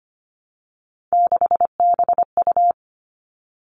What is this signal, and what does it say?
Morse code, heard as a keyed tone, about 25 wpm, 700 Hz. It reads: 6BU
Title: All Cut Numbers - 3-Digit 25wpm